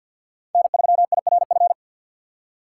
Morse code, heard as a keyed tone, about 48 wpm, 700 Hz. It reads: N3IRF
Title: USA Call Signs - 5 Characters - 6x Character Spacing - 50wpm